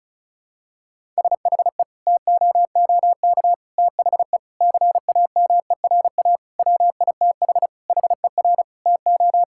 Morse code, heard as a keyed tone, about 36 wpm, 700 Hz. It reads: SHE TOOK THE CAMERA WITH HER TO
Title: Sentences from Top 1000 Words - Review Part 5 - 35wpm